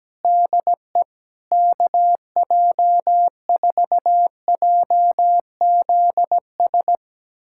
Morse code, heard as a keyed tone, about 17 wpm, 700 Hz. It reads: DE KJ4JZS
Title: USA Callsign DE Callsign - 6 Characters - 17wpm